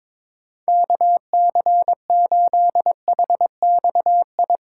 Morse code, heard as a keyed tone, about 21 wpm, 700 Hz. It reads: KC8HXI
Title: USA Call Signs - 3x Character Spacing - 22wpm